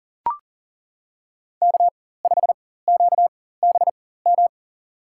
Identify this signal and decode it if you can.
Morse code, heard as a keyed tone, about 40 wpm, spaced as for 19 wpm, 700 Hz. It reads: K5QBM
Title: USA Call Signs - 5 Characters - 4x Character Spacing - 40wpm